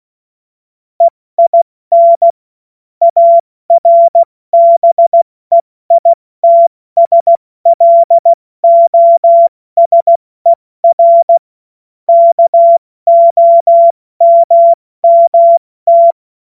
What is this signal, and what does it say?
Morse code, heard as a keyed tone, about 16 wpm, 680 Hz. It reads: EIN ARBEITSLOSER KOMMT
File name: Morsekurs / Witze / Witz 29. November 2022 70 F80